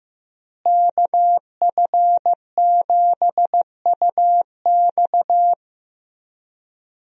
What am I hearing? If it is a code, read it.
Morse code, heard as a keyed tone, about 15 wpm, 700 Hz. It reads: KF7UX
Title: USA Call Signs - 5 Characters - Repeated 3x - 15wpm